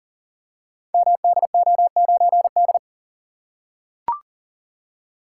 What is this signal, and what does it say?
Morse code, heard as a keyed tone, about 44 wpm, 690 Hz. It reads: MDO9D
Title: Code Groups - 5 Letter:Numbers - 40wpm